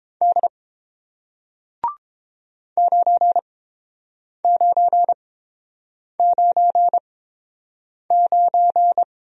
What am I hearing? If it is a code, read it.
Morse code, heard as a keyed tone, about 30 wpm, spaced as for 7 wpm, 700 Hz. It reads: D 9999